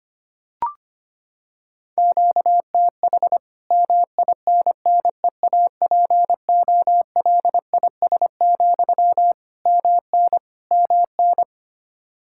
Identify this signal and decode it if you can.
Morse code, heard as a keyed tone, about 25 wpm, 700 Hz. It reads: QTH MINNEAPOLIS, MN MN
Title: Practice QSO - Exchange 1 - Beginning - 25wpm